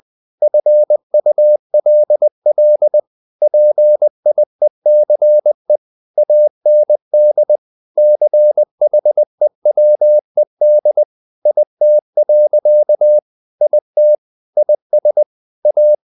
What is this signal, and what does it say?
Morse code, heard as a keyed tone, about 20 wpm, 600 Hz. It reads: FULL PIECE AND CHEWED IT. IT IS A
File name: the old man and the sea / the old man and the sea part 1 20wpm